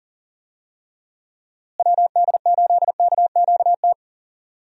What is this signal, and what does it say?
Morse code, heard as a keyed tone, about 39 wpm, 700 Hz. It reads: WD8KQT
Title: USA Call Signs - 6 Characters - Repeated 3x - 40wpm